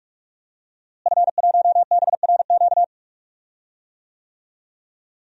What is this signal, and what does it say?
Morse code, heard as a keyed tone, about 46 wpm, 700 Hz. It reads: F1BRQ
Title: DXCC-100 Call Signs - 5 Characters - Repeated 3x - 45wpm